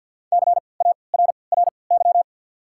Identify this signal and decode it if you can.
Morse code, heard as a keyed tone, about 50 wpm, 690 Hz. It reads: C A R R Y
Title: 3x Character Spacing - Top 300 Words - 50wpm